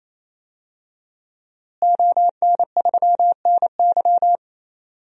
Morse code, heard as a keyed tone, about 28 wpm, 700 Hz. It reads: ON3NY
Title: DXCC-100 Call Signs - 5 Characters - Repeated 3x - 28wpm